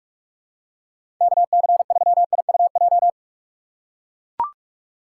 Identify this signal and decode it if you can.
Morse code, heard as a keyed tone, about 42 wpm, 700 Hz. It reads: KC3IUJ
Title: USA Call Signs - 6 Characters - 4x Character Spacing - 45wpm